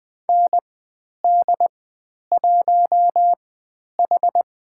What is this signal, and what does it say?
Morse code, heard as a keyed tone, about 20 wpm, spaced as for 10 wpm, 700 Hz. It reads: ND1H
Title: 102 Lesson 17g - Letter u - Call Signs 20wpm